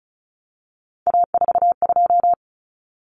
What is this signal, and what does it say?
Morse code, heard as a keyed tone, about 34 wpm, 700 Hz. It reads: A42
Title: Common Cut Numbers - 3-Digit 35wpm